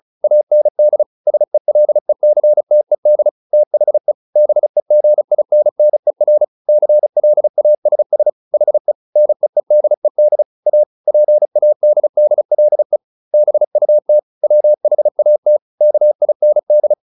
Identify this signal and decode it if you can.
Morse code, heard as a keyed tone, about 35 wpm, 600 Hz. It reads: AND SELECTED THE BEGINNER CLASS HE NEEDED A PADDLE BUT WHAT KIND